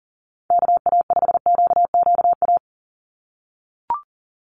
Morse code, heard as a keyed tone, about 40 wpm, 700 Hz. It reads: KA5QQA
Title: NA SOTA Chaser Callsigns 40wpm